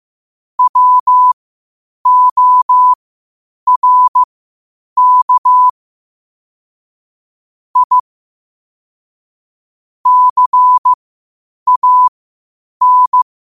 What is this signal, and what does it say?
Morse code, heard as a keyed tone, about 15 wpm, spaced as for 8 wpm, 1000 Hz. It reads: WORK I CAN
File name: qotdfiles / 8wpm